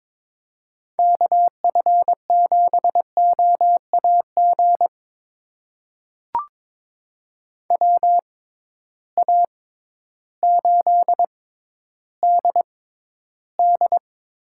Morse code, heard as a keyed tone, about 22 wpm, spaced as for 12 wpm, 700 Hz. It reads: KF7OAG WA8DD
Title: USA Call Signs - 6 Characters - 6x Character Spacing - 22wpm